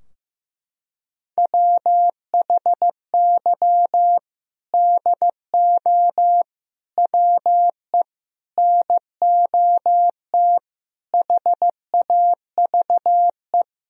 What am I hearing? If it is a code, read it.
Morse code, heard as a keyed tone, about 16 wpm, 700 Hz. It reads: WHY DO WE NOT HAVE
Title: Sentences from Top 1000 Words - Review Part 3 - 15wpm